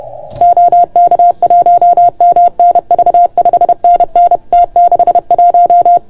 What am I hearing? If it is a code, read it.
Morse code, heard as a keyed tone, about 29 wpm, 690 Hz. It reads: OK1MNV 5NN T61
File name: Archiv / CUC381 / OK1FGD CUC 381